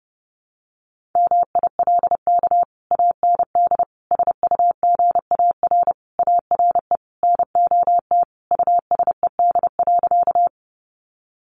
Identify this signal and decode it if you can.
Morse code, heard as a keyed tone, about 30 wpm, 700 Hz. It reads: MILK AND SUGAR ARE NOT USED.
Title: Sentences from Top 1500 Words 30wpm